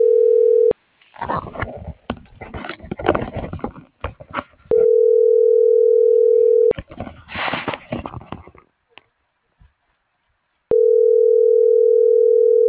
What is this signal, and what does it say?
Ringback tone (North America), 440 + 480 Hz.